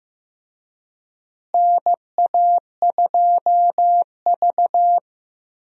Morse code, heard as a keyed tone, about 15 wpm, 700 Hz. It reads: NA2V
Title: USA Call Signs - 4 Characters - Repeated 3x - 15wpm